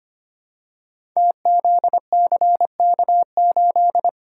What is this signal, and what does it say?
Morse code, heard as a keyed tone, about 25 wpm, 700 Hz. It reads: TZCK8
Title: Code Groups - 5 Letter:Numbers - 25wpm